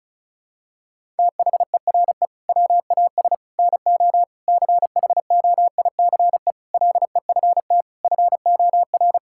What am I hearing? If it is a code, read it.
Morse code, heard as a keyed tone, about 34 wpm, 700 Hz. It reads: THERE WAS NO CHOICE LEFT FOR
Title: Sentences from Top 1000 Words - Review Part 6 - 35wpm